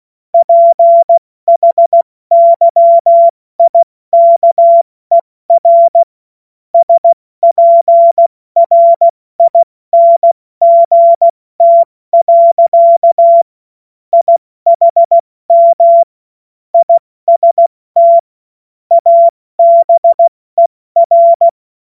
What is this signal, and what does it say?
Morse code, heard as a keyed tone, about 16 wpm, 680 Hz. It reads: PHYIKER SPRINGT. IHM IST ABER
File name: Morsekurs / Witze / Witz 07. Dezember 2021, 70 F80